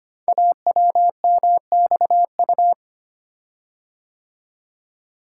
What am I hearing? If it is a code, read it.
Morse code, heard as a keyed tone, about 25 wpm, 700 Hz. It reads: AWMXU
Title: Code Groups - 5 Letters - 25wpm